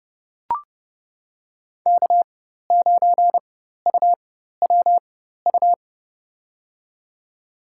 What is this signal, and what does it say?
Morse code, heard as a keyed tone, about 30 wpm, spaced as for 14 wpm, 700 Hz. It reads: K9UWU
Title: USA Call Signs - 5 Characters - 4x Character Spacing - 30wpm